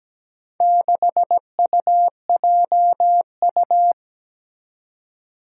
Morse code, heard as a keyed tone, about 17 wpm, 700 Hz. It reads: 6UJU
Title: Four Challenging Letter-Number Combinations 17wpm